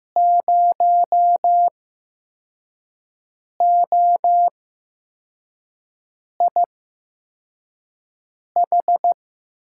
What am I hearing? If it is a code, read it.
Morse code, heard as a keyed tone, about 15 wpm, spaced as for 4 wpm, 700 Hz. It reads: 0OIH
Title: USA Call Signs - 5 Characters - 8x Character Spacing - 15wpm